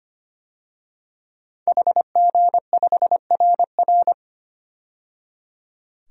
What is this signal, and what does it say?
Morse code, heard as a keyed tone, about 25 wpm, 700 Hz. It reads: HG5RR
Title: DXCC-100 Call Signs - 5 Characters - 25wpm